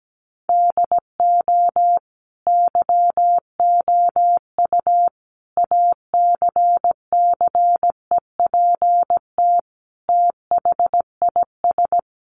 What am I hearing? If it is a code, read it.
Morse code, heard as a keyed tone, about 17 wpm, 700 Hz. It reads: DO YOU ACCEPT THIS
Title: Sentences from Top 1300 Words 17wpm